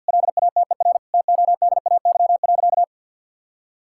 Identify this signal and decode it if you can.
Morse code, heard as a keyed tone, about 50 wpm, 700 Hz. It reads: LATER TODAY.
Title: Sentences from Top 1000 Words - Review Part 6 - 50wpm